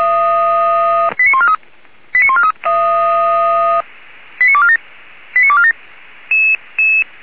Five-tone selective call (ZVEI), heard as quick runs of five tones, tones of about 70 ms. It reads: 89143 89143 89247 89247